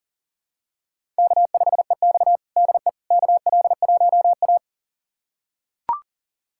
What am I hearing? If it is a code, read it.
Morse code, heard as a keyed tone, about 37 wpm, 700 Hz. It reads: K5EX DE KL1A
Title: USA Callsign DE Callsign - 4 Characters - 40wpm